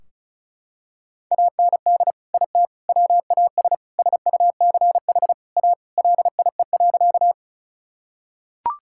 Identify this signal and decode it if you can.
Morse code, heard as a keyed tone, about 36 wpm, 700 Hz. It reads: AND IT WAS SUCH A LIE.
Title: Sentences from Top 1600 Words 35wpm